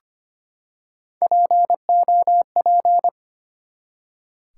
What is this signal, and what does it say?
Morse code, heard as a keyed tone, about 25 wpm, 700 Hz. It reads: POP
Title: Three Letter Words - 25wpm